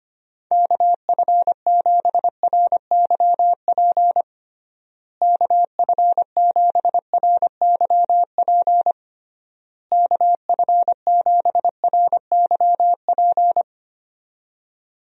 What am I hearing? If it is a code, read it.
Morse code, heard as a keyed tone, about 25 wpm, 700 Hz. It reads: KF7RYP KF7RYP KF7RYP
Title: USA Call Signs - 6 Characters - Repeated 3x - 25wpm